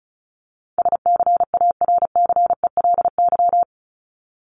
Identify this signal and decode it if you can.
Morse code, heard as a keyed tone, about 34 wpm, 700 Hz. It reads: SCARCELY
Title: LY Suffix - Top 100 Words with Base Words 35wpm